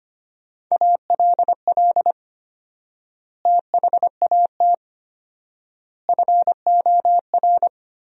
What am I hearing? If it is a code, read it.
Morse code, heard as a keyed tone, about 25 wpm, 700 Hz. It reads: ALL THAT FOR
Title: 4x Word-Spacing - Sentences from Top 500 Words - 25wpm